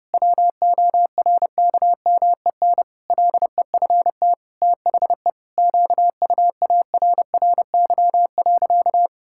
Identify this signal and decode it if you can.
Morse code, heard as a keyed tone, about 32 wpm, 700 Hz. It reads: WORKMEN LEFT THE QUARRY.
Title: Pangram Challenge v2 - 30wpm